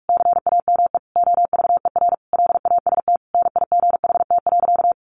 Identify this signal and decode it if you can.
Morse code, heard as a keyed tone, about 44 wpm, 700 Hz. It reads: CAME OVER LAST NIGHT.
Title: Sentences from Top 300 Words 45wpm